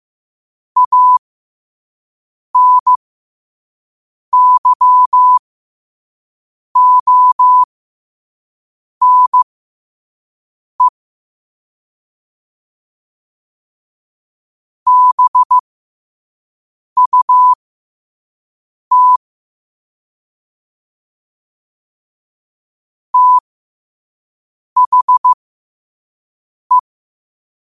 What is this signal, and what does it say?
Morse code, heard as a keyed tone, about 15 wpm, spaced as for 5 wpm, 1000 Hz. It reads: ANYONE BUT THE
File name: qotdfiles / 5wpm